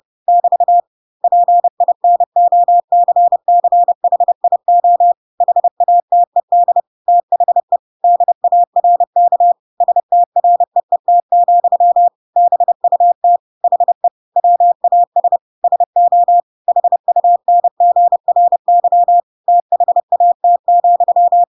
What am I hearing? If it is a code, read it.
Morse code, heard as a keyed tone, about 30 wpm, 700 Hz. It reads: <BT> PINOCCHIO HATED THE DARK STREET, BUT HE WAS SO HUNGRY THAT,
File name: Pinocchio / Pinocchio ch6 30 30 ik1hge